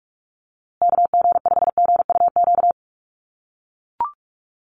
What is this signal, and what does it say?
Morse code, heard as a keyed tone, about 44 wpm, 700 Hz. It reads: KG5GUQ
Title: USA Call Signs Repeated 3x 45wpm